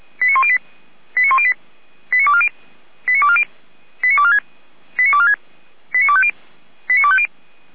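Five-tone selective call (ZVEI), heard as quick runs of five tones, tones of about 70 ms. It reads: 89108 89108 89240 89240 89246 89246 89250 89250
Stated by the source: ZVEI five-tone selective call received via SDR, recorded on the Feuerwehr Orlamünde fire brigade audio server.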